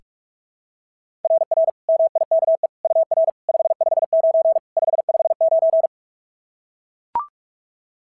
Morse code, heard as a keyed tone, about 44 wpm, 690 Hz. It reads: RR MIKE UR 559 559
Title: SOTA QSO Practice 45wpm